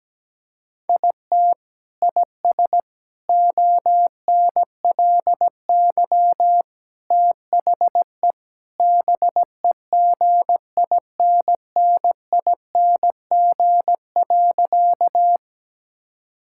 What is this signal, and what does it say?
Morse code, heard as a keyed tone, about 17 wpm, 700 Hz. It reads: IT IS ONLY THE BEGINNING.
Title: Sentences from Top 1000 Words - Review Part 5 - 17wpm